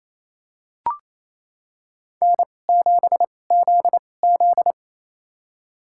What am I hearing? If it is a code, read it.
Morse code, heard as a keyed tone, about 28 wpm, spaced as for 20 wpm, 700 Hz. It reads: N7ZZ
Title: USA Call Signs - 4 Characters - 2x Character Spacing - 28wpm